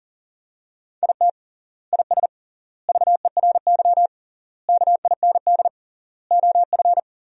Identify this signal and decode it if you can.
Morse code, heard as a keyed tone, about 37 wpm, 700 Hz. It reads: IT IS VERY KIND OF
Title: 3x Word-Spacing - Sentences from Top 200 Words - 40wpm